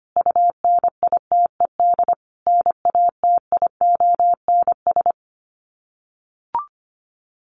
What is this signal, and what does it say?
Morse code, heard as a keyed tone, about 25 wpm, 700 Hz. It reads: UNITED NATIONS
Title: Sets of 5 Words Encore2 25wpm